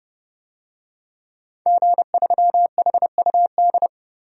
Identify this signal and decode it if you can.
Morse code, heard as a keyed tone, about 32 wpm, 700 Hz. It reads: G3HUD
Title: DXCC-100 Call Signs - 5 Characters - 30wpm